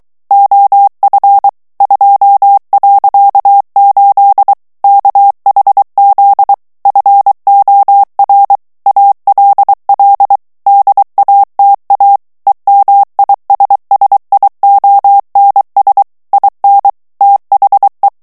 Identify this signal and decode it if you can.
Morse code, heard as a keyed tone, about 23 wpm, 800 Hz. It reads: OF 2.8 KHZ FOR ALL DATA EMISSIONS IN THE
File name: ANGLAIS texte clair 3 vitesse 22 mots minute